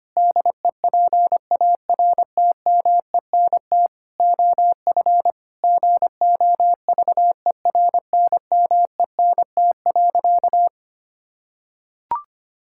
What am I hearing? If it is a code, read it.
Morse code, heard as a keyed tone, about 25 wpm, 700 Hz. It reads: DEPARTMENT OF GOVERNMENT.
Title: Sentences from Top 1000 Words - Review Part 5 - 25wpm